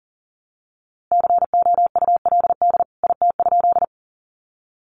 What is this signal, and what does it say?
Morse code, heard as a keyed tone, about 40 wpm, 700 Hz. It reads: COULD IT?
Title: A Sound Of Thunder 40wpm